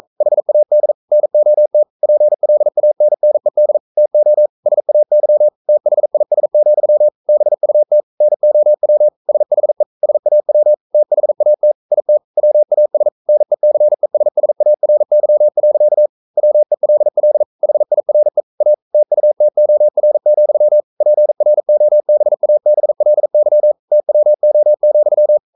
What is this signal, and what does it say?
Morse code, heard as a keyed tone, about 42 wpm, 600 Hz. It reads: HAD NOT PLANNED TO SAY THIS, BUT NOW SHE SAW THAT IT WAS NECESSARY. WELL HIRE A TUTOR, PROBABLY TWO,
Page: Romanen Never af Ken Follet på engelsk.